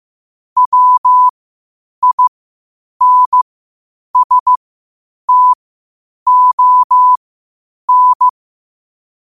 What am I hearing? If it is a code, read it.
Morse code, heard as a keyed tone, about 15 wpm, spaced as for 8 wpm, 1000 Hz. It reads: WINSTON
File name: qotdfiles / 8wpm